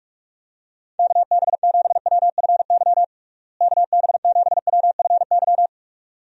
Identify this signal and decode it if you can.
Morse code, heard as a keyed tone, about 46 wpm, 700 Hz. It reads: KB7WFY KB7WFY
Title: USA Call Signs - 6 Characters - Repeated 2x - 45wpm